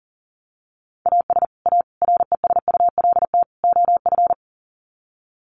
Morse code, heard as a keyed tone, about 40 wpm, 700 Hz. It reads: AS A RESULT OF
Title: Sets of 4 Words 40wpm